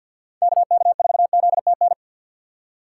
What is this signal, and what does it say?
Morse code, heard as a keyed tone, about 46 wpm, 700 Hz. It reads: KK4ZTN
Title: USA Call Signs - 6 Characters - 6x Character Spacing - 50wpm